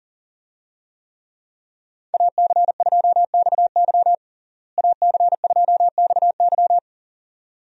Morse code, heard as a keyed tone, about 37 wpm, 700 Hz. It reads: AC2XY AC2XY
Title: USA Call Signs - 5 Characters - Repeated 2x - 40wpm